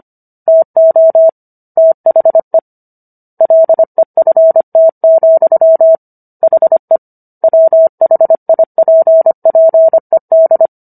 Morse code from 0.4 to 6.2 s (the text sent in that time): TO THE LEFT,